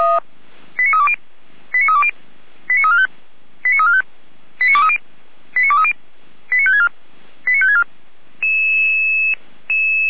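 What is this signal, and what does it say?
Five-tone selective call (ZVEI), heard as quick runs of five tones, tones of about 70 ms. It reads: 89320 89320 89345 89345 89230 89230 89564 89564
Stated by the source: ZVEI five-tone selective call received with an SDR.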